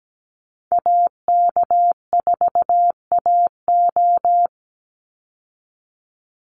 Morse code, heard as a keyed tone, about 17 wpm, 700 Hz. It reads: AK4AO
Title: Call Signs Repeated 3x - 17wpm